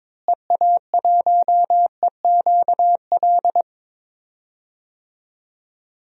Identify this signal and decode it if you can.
Morse code, heard as a keyed tone, about 21 wpm, 700 Hz. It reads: EA1EQL
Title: DXCC-100 Call Signs - 6 Characters - Repeated 2x - 22wpm